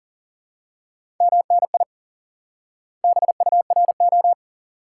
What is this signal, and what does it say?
Morse code, heard as a keyed tone, about 40 wpm, 700 Hz. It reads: MNI BURO
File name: IQE Course 3 - Lesson 89 - YAGI - Two Elements - 40wpm